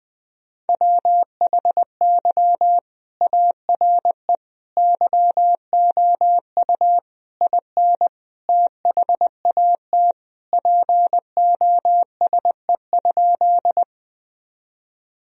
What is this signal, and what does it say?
Morse code, heard as a keyed tone, about 21 wpm, 700 Hz. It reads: WHY ARE YOU IN THAT POSE?
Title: Sentences from Top 1000 Words - Review Part 2 - 20wpm